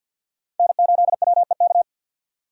Morse code, heard as a keyed tone, about 49 wpm, 700 Hz. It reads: N8WEK
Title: USA Call Signs - 5 Characters - 50wpm